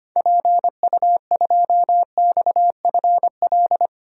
Morse code, heard as a keyed tone, about 25 wpm, 700 Hz. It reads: PU2XFL
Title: DXCC-100 Call Signs - 6 Characters - Repeated 2x - 25wpm